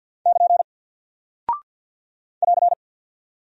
Morse code, heard as a keyed tone, about 48 wpm, 700 Hz. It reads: <KN> <AR>
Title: Single Character (Letter:Number:Punctuation:Prosign) 50wpm